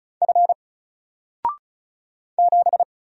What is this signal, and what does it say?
Morse code, heard as a keyed tone, about 36 wpm, 700 Hz. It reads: F 7
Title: Single Letter-Number 35wpm